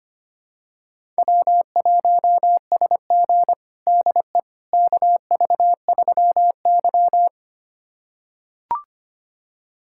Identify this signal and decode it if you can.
Morse code, heard as a keyed tone, about 25 wpm, 700 Hz. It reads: W1SG DE KV3Y
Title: USA Callsign DE Callsign - 4 Characters - 25wpm